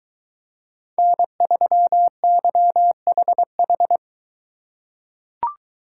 Morse code, heard as a keyed tone, about 23 wpm, 700 Hz. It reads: N3YHH